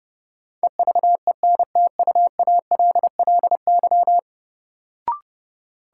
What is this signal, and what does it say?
Morse code, heard as a keyed tone, about 30 wpm, 700 Hz. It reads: EVENTUALLY
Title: Sentences from Top 1000 Words - Review Part 6 - New Word Drill - 30wpm